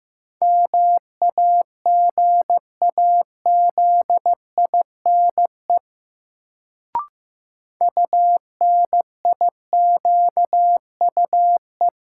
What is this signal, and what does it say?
Morse code, heard as a keyed tone, about 16 wpm, 700 Hz. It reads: MAGAZINE UNIQUE
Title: Sentences from Top 1000 Words - Review Part 7 - New Word Drill - 15wpm